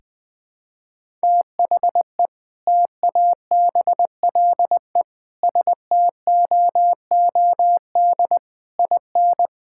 Morse code, heard as a keyed tone, about 21 wpm, 700 Hz. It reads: THE TABLE STOOD IN
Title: Sentences with 4-Word Phrases - 20wpm